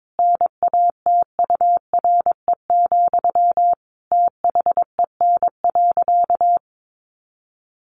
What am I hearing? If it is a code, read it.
Morse code, heard as a keyed tone, about 22 wpm, 700 Hz. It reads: NATURE, THEN.
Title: Sentences from Top 1200 Words 22wpm